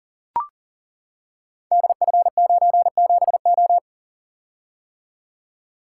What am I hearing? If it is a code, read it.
Morse code, heard as a keyed tone, about 39 wpm, 700 Hz. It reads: DF97O
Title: Code Groups - 5 Letter:Numbers - 40wpm